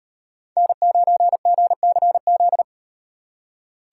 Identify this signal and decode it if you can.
Morse code, heard as a keyed tone, about 38 wpm, 700 Hz. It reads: N9GCZ